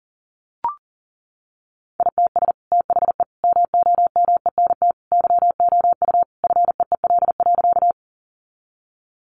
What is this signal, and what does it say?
Morse code, heard as a keyed tone, about 40 wpm, 700 Hz. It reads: ITS THE MOMENT YOU FEEL.
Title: Sentences from Top 1000 Words 40wpm